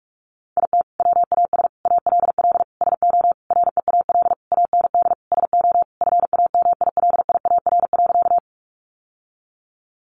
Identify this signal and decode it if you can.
Morse code, heard as a keyed tone, about 44 wpm, 700 Hz. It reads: IT WAS ALL SO REAL AND SO FAMILIAR.
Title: Sentences from Top 1000 Words 45wpm